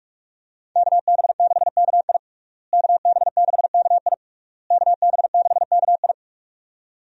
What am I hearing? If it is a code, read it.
Morse code, heard as a keyed tone, about 47 wpm, 700 Hz. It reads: KB6KI KB6KI KB6KI
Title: USA Call Signs - 5 Characters - Repeated 3x - 45wpm